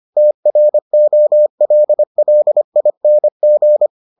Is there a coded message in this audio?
Yes: Morse code from 0.2 to 3.9 s, heard as a keyed tone, about 25 wpm, 600 Hz. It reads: TROLLING